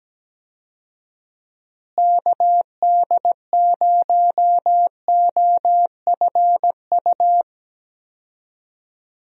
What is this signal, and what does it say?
Morse code, heard as a keyed tone, about 17 wpm, 700 Hz. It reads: KD0OFU
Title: USA Call Signs - 6 Characters - Repeated 2x - 17wpm